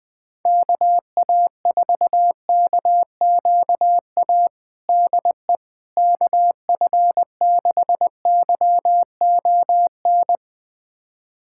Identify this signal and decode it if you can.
Morse code, heard as a keyed tone, about 20 wpm, 700 Hz. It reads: KA4KQA DE KF6YON
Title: USA Callsign DE Callsign 20wpm